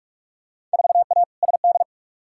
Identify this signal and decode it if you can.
Morse code, heard as a keyed tone, about 42 wpm, 700 Hz. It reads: 4A SD
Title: Field Day Exchanges - 45wpm